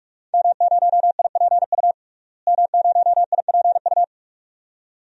Morse code, heard as a keyed tone, about 43 wpm, 700 Hz. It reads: M0IPU M0IPU
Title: DXCC-100 Call Signs - 5 Characters - Repeated 2x - 45wpm